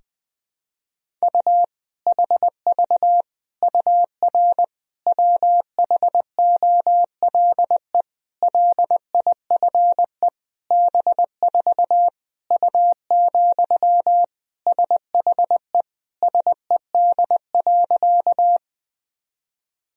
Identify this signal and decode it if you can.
Morse code, heard as a keyed tone, about 20 wpm, 700 Hz. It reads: U HV UR WHOLE LIFE B4 U, SHE SED.
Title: Sentences from Top 400 Words and Abbreviations 20wpm